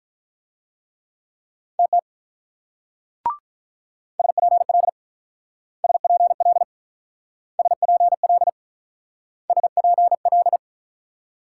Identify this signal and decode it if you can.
Morse code, heard as a keyed tone, about 45 wpm, 700 Hz. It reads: TT SPL SPL SPL SPL